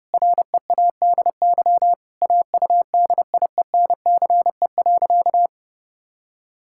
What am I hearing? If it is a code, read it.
Morse code, heard as a keyed tone, about 31 wpm, 700 Hz. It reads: READY AUDIENCE.
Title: Sentences from Top 1000 Words - Review Part 5 - 30wpm